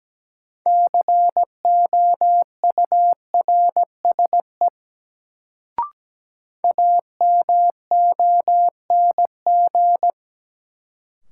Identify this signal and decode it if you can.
Morse code, heard as a keyed tone, about 17 wpm, 700 Hz. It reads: COURSE AMONG
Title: Sentences from Top 500 Words - New Word Drill - 17wpm